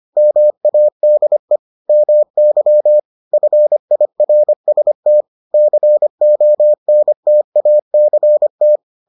Morse code, heard as a keyed tone, about 25 wpm, 600 Hz. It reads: MADE MY FIRST CONTACT